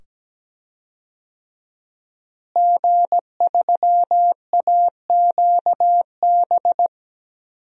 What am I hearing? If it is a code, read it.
Morse code, heard as a keyed tone, about 17 wpm, 700 Hz. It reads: G3AQB
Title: DXCC-100 Call Signs - 5 Characters - Repeated 2x - 17wpm